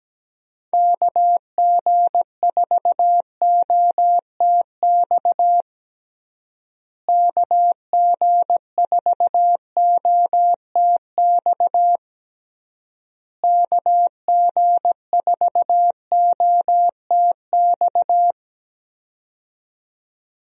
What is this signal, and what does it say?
Morse code, heard as a keyed tone, about 17 wpm, 700 Hz. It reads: KG4OTX KG4OTX KG4OTX
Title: USA Call Signs - 6 Characters - Repeated 3x - 17wpm